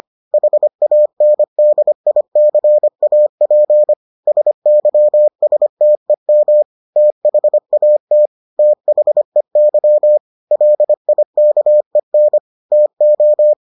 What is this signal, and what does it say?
Morse code, heard as a keyed tone, about 25 wpm, 600 Hz. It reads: HANDICAP SYSTEM THAT THEY LIKEN TO